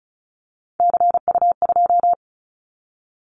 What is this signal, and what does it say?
Morse code, heard as a keyed tone, about 36 wpm, 700 Hz. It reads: CU2
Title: Callsign Prefixes for Top 100 DXCC - 35wpm